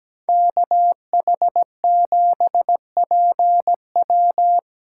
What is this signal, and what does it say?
Morse code, heard as a keyed tone, about 17 wpm, 700 Hz. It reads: KH7PW
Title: USA Callsign DE Callsign - 5 Characters - 17wpm